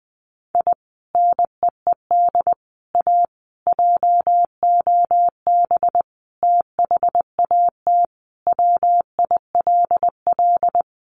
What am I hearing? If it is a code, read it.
Morse code, heard as a keyed tone, about 20 wpm, 700 Hz. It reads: I NEED A JOB THAT WILL
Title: Sentences from Top 1500 Words 20wpm